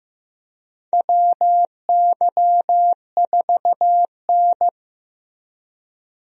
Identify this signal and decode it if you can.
Morse code, heard as a keyed tone, about 15 wpm, 700 Hz. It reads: WY4N
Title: USA Call Signs - 4 Characters - Repeated 2x - 15wpm